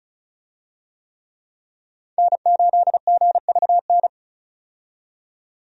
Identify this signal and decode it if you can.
Morse code, heard as a keyed tone, about 36 wpm, 700 Hz. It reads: N8GVN
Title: USA Call Signs - 5 Characters - 35wpm